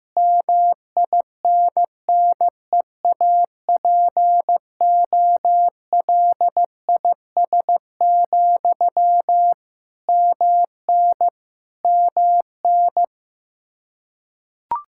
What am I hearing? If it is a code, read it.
Morse code, heard as a keyed tone, about 15 wpm, 700 Hz. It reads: MINNEAPOLIS, MN MN
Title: Practice QSO - Exchange 1 - Beginning - 15wpm